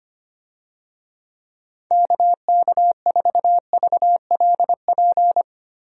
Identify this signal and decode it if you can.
Morse code, heard as a keyed tone, about 25 wpm, 700 Hz. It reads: KK4VLP
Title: USA Call Signs - 6 Characters - Repeated 3x - 25wpm